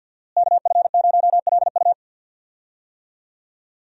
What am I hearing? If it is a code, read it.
Morse code, heard as a keyed tone, about 52 wpm, 700 Hz. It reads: KF0LU
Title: USA Call Signs - 5 Characters - Repeated 3x - 50wpm